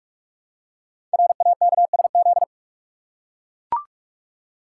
Morse code, heard as a keyed tone, about 44 wpm, 700 Hz. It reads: RAKSZ
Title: Code Groups - 5 Letter:Numbers - 45wpm